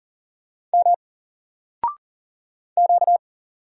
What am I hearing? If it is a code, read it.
Morse code, heard as a keyed tone, about 40 wpm, 700 Hz. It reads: M Q
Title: Single Character (Letter:Number:Punctuation:Prosign) 40wpm